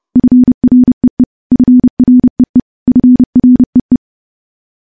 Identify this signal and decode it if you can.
Morse code, heard as a keyed tone, about 30 wpm, 250 Hz. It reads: FREE FREE FREE
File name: morse / morse-training-30WPM